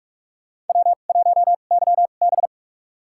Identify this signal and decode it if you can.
Morse code, heard as a keyed tone, about 42 wpm, spaced as for 31 wpm, 700 Hz. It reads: W1YB
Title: USA Call Signs - 2x Character Spacing - 45wpm